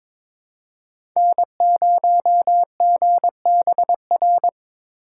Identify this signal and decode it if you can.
Morse code, heard as a keyed tone, about 22 wpm, 700 Hz. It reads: N0GBR
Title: USA Call Signs - 5 Characters - 8x Character Spacing - 22wpm